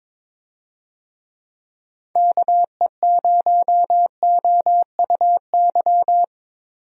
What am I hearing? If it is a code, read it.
Morse code, heard as a keyed tone, about 22 wpm, 700 Hz. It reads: KE0OUY
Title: USA Call Signs - 6 Characters - Repeated 3x - 22wpm